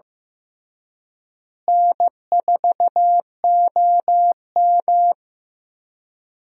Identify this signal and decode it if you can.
Morse code, heard as a keyed tone, about 15 wpm, 700 Hz. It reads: N4OM
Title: USA Call Signs - 4 Characters - Repeated 2x - 15wpm